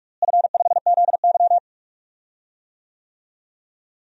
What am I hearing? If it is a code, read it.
Morse code, heard as a keyed tone, about 46 wpm, 700 Hz. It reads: F5ZY
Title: DXCC-100 Call Signs - 4 Characters - Repeated 2x - 45wpm